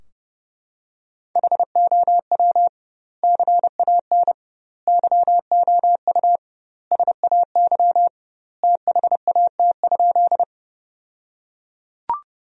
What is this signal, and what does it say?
Morse code, heard as a keyed tone, about 32 wpm, 700 Hz. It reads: HOW CAN YOU SAY THAT?
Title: 2x Word-Spacing - Sentences from Top 200 Words - 30wpm